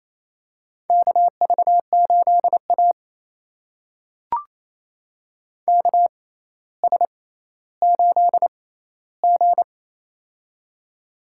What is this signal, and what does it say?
Morse code, heard as a keyed tone, about 28 wpm, spaced as for 14 wpm, 700 Hz. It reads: KV8A KS8G
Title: USA Call Signs - 4 Characters - 6x Character Spacing - 28wpm